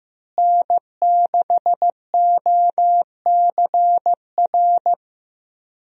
Morse code, heard as a keyed tone, about 15 wpm, 700 Hz. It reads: N6OCR
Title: USA Call Signs - 5 Characters - Repeated 2x - 15wpm